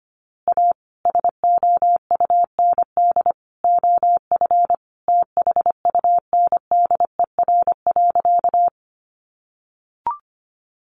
Morse code, heard as a keyed tone, about 25 wpm, 700 Hz. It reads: A SOUND OF THUNDER.
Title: A Sound Of Thunder 25wpm